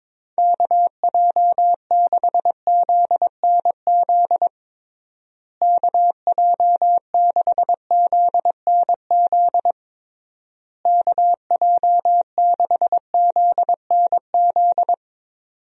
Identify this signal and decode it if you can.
Morse code, heard as a keyed tone, about 22 wpm, 700 Hz. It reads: KJ6ZNZ KJ6ZNZ KJ6ZNZ
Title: USA Call Signs - 6 Characters - Repeated 3x - 22wpm